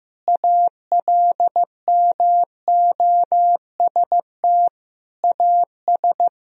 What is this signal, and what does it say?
Morse code, heard as a keyed tone, about 16 wpm, 700 Hz. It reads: ALMOST AS
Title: Sentences from Top 1000 Words - Review Part 4 - 15wpm